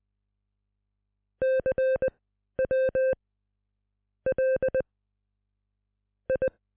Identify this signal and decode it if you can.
Morse code, heard as a keyed tone, about 20 wpm, spaced as for 7 wpm, 520 Hz. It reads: CW LI